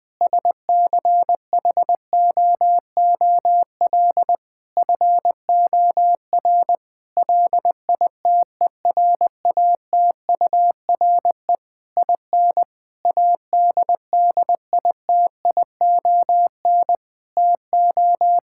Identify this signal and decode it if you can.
Morse code, heard as a keyed tone, about 20 wpm, 700 Hz. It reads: SCHOOL FOR LITERATURE IN ADDITION TO
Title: Sentences from Top 1000 Words - Review Part 6 - 20wpm